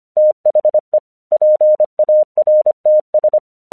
Morse code, heard as a keyed tone, about 26 wpm, 610 Hz. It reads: THE PARTS
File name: 1 English 25wpm / AQC23